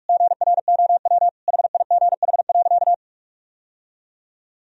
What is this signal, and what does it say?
Morse code, heard as a keyed tone, about 46 wpm, 700 Hz. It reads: GROW HIGH.
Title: Sentences from Top 1000 Words - Review Part 1 - 45wpm